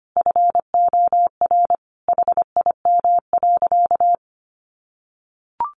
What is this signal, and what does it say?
Morse code, heard as a keyed tone, about 25 wpm, 700 Hz. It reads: FOR HIM.
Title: Sentences from Top 1400 Words 25wpm